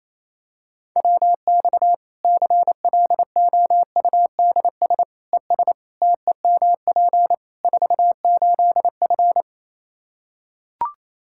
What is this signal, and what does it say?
Morse code, heard as a keyed tone, about 27 wpm, 700 Hz. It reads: WX CLOUDS ES TEMP 48F
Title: Practice QSO - Exchange 2 - Info - 28wpm